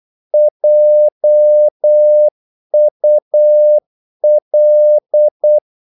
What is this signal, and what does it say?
Morse code, heard as a keyed tone, about 8 wpm, 600 Hz. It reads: JUL
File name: Korte ord DK 8wpm